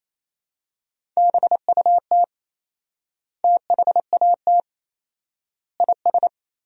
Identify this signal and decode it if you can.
Morse code, heard as a keyed tone, about 28 wpm, 700 Hz. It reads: BUT THAT IS
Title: 4x Word-Spacing - Sentences from Top 500 Words - 28wpm